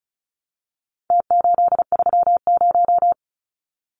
Morse code, heard as a keyed tone, about 34 wpm, 700 Hz. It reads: T830
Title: Numbers - Contest Style - 4-Digit 35wpm